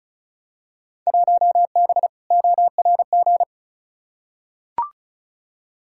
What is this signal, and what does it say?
Morse code, heard as a keyed tone, about 36 wpm, 700 Hz. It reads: 1B ORG
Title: Field Day Exchanges Encore - 35wpm